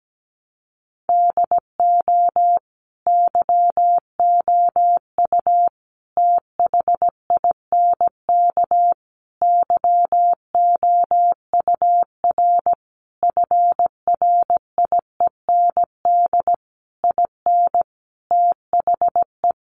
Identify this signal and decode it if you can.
Morse code, heard as a keyed tone, about 17 wpm, 700 Hz. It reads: DO YOU THINK YOUR FRIEND IN THE
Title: Sentences from Top 1100 Words 17wpm